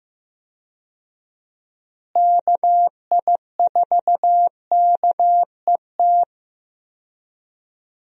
Morse code, heard as a keyed tone, about 15 wpm, 700 Hz. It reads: KI4KET
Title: USA Call Signs - 6 Characters - Repeated 3x - 15wpm